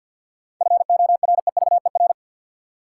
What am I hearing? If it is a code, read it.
Morse code, heard as a keyed tone, about 53 wpm, 700 Hz. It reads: FOREVER
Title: Seven Letter Words - 50wpm